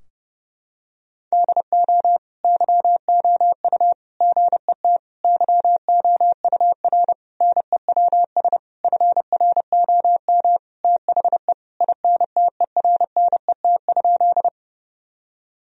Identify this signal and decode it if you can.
Morse code, heard as a keyed tone, about 32 wpm, 700 Hz. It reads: DO YOU GET YOUR NEWS FROM THE INTERNET?
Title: Sentences from Top 1000 Words - Review Part 4 - 30wpm